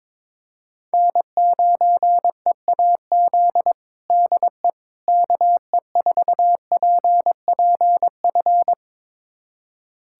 Morse code, heard as a keyed tone, about 22 wpm, 700 Hz. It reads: N9EAZ DE KE4PPF
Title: USA Callsign DE Callsign 22wpm